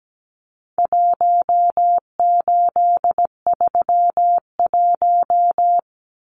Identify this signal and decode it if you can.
Morse code, heard as a keyed tone, about 17 wpm, 700 Hz. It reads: 1831
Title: Numbers - Contest Style - 4-Digit 17wpm